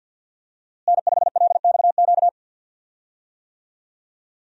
Morse code, heard as a keyed tone, about 49 wpm, 700 Hz. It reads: N5LXQ
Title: USA Call Signs - 5 Characters - Repeated 2x - 50wpm